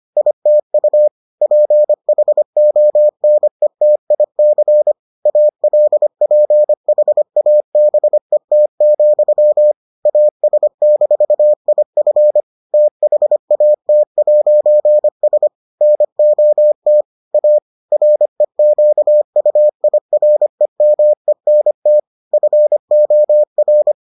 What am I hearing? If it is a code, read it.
Morse code, heard as a keyed tone, about 25 wpm, 600 Hz. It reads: ITU PHONETIC ALPHABET, AS-IF THAT'S NOT A REQUIREMENT FOR